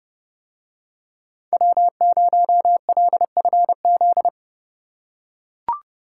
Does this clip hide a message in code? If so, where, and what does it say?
Yes: Morse code from 1.5 to 4.3 s, heard as a keyed tone, about 29 wpm, 700 Hz. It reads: W0LFZ